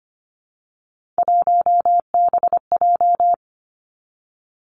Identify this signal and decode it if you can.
Morse code, heard as a keyed tone, about 25 wpm, 700 Hz. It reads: 1BJ
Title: Three Challenging Letter-Number Combinations 25wpm